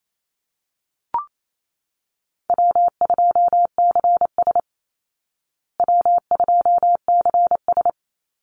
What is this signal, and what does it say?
Morse code, heard as a keyed tone, about 28 wpm, 700 Hz. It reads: W2CS W2CS
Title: Easy Call Signs Repeated 3x - 28wpm